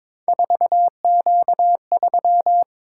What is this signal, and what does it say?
Morse code, heard as a keyed tone, about 22 wpm, 700 Hz. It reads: 4Q3
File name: practice / 22wpm / Three Letter-Number Combinations - 22wpm